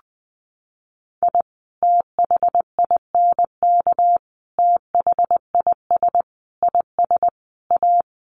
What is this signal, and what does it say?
Morse code, heard as a keyed tone, about 20 wpm, 700 Hz. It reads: I THINK THIS IS A
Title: Sets of 5 Words 20wpm